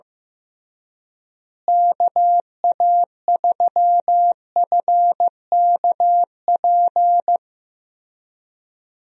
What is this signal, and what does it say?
Morse code, heard as a keyed tone, about 15 wpm, 700 Hz. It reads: KA3FKP
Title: USA Call Signs - 6 Characters - Repeated 3x - 15wpm